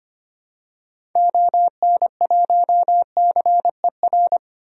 Morse code, heard as a keyed tone, about 25 wpm, 700 Hz. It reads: ON1CER
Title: DXCC-100 Call Signs - 6 Characters - 25wpm